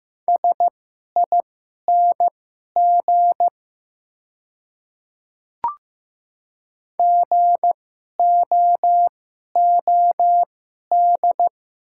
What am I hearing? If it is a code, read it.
Morse code, heard as a keyed tone, about 15 wpm, spaced as for 11 wpm, 700 Hz. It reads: SING GOOD
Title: 2x Character Spacing - Top 500 Words - 15wpm